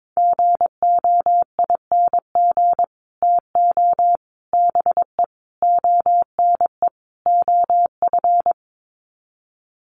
Morse code, heard as a keyed tone, about 22 wpm, 700 Hz. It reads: GOING TO BE ONE OF
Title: Sets of 5 Words Encore2 22wpm